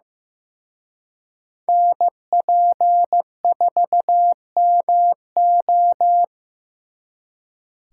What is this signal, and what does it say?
Morse code, heard as a keyed tone, about 16 wpm, 700 Hz. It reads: NP4MO
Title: Speed Racing - USA 2x2 Call Signs 15wpm